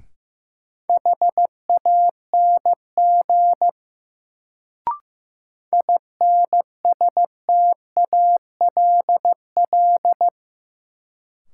Morse code, heard as a keyed tone, about 15 wpm, 700 Hz. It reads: HANG INSTALL
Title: Sentences from Top 1600 Words - New Word Drill Alt - 15wpm